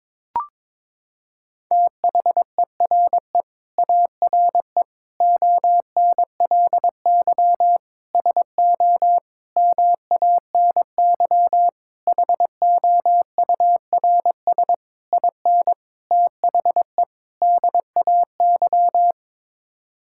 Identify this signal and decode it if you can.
Morse code, heard as a keyed tone, about 22 wpm, 700 Hz. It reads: THERE ARE ONLY SO MANY HOURS IN THE DAY
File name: Sets of 9 Words - 22wpm